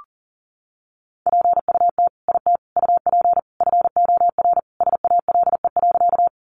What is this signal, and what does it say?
Morse code, heard as a keyed tone, about 37 wpm, 700 Hz. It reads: PUT IT UP FOR SALE.
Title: Sentences from Top 1400 Words 40wpm